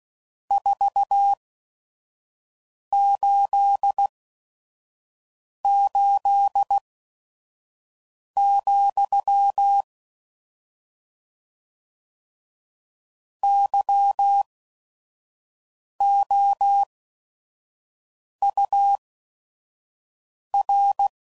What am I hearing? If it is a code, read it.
Morse code, heard as a keyed tone, about 16 wpm, spaced as for 5 wpm, 800 Hz. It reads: 488, YOUR
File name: CW Code Groups @5WPM / RST's / Your RST's